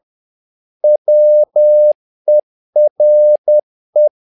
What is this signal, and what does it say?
Morse code, heard as a keyed tone, about 10 wpm, 600 Hz. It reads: WERE
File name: learning center / tower trouble / tower trouble 10wpm